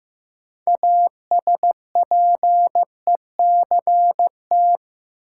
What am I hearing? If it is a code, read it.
Morse code, heard as a keyed tone, about 15 wpm, 700 Hz. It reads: ASPECT
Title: Sentences from Top 1300 Words - New Word Drill - 15wpm